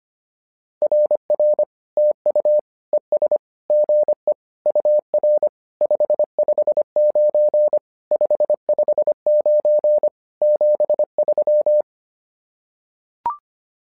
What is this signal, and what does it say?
Morse code, heard as a keyed tone, about 25 wpm, 600 Hz. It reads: RR TU ES GE UR 559 559 73
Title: SOTA QSO Practice 25wpm